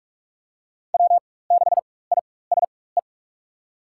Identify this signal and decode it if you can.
Morse code, heard as a keyed tone, about 46 wpm, spaced as for 21 wpm, 700 Hz. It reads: W6ISE
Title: USA Call Signs - 5 Characters - 4x Character Spacing - 45wpm